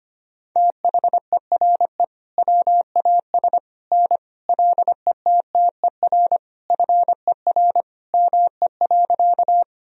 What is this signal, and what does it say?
Morse code, heard as a keyed tone, about 25 wpm, 700 Hz. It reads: THERE WAS N LETTER FER ME.
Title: Sentences from Top 300 Words and Abbreviations 25wpm